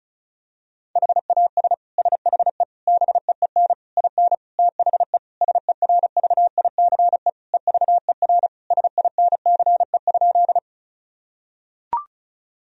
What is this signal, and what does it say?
Morse code, heard as a keyed tone, about 34 wpm, 700 Hz. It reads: HAS SHE BEEN IN THE SERVICE EVER SINCE?
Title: Sentences from Top 1000 Words - Review Part 4 - 35wpm